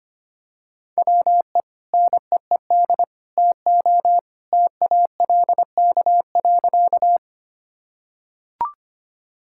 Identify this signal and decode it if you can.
Morse code, heard as a keyed tone, about 25 wpm, 700 Hz. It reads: WE NEED TO TALK.
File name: practice / 25wpm / Sentences with 3-Word Phrases - 25wpm